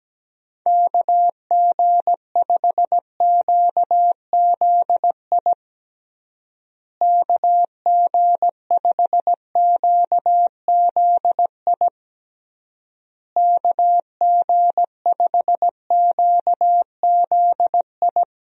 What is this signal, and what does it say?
Morse code, heard as a keyed tone, about 17 wpm, 700 Hz. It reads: KG5QZI KG5QZI KG5QZI
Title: USA Call Signs - 6 Characters - Repeated 3x - 17wpm